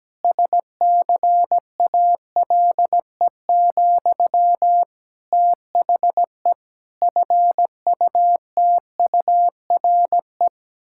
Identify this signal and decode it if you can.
Morse code, heard as a keyed tone, about 17 wpm, 700 Hz. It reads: SCALE, THE FUTURE
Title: Sentences from Top 1000 Words - Review Part 5 - 17wpm